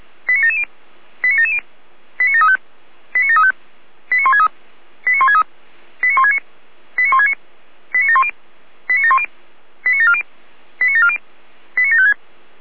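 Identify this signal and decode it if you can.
Five-tone selective call (ZVEI), heard as quick runs of five tones, tones of about 70 ms. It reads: 89770 89770 89735 89735 89163 89163 89169 89169 89710 89710 89740 89740 89756
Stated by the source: ZVEI five-tone selective call received via SDR, address 89720.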